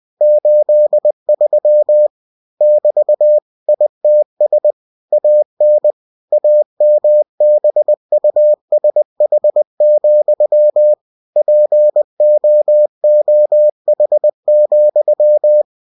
Morse code, heard as a keyed tone, about 20 wpm, 600 Hz. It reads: 83 <BT> ITS AN AMBUSH, POOH,
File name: winnie-20wpm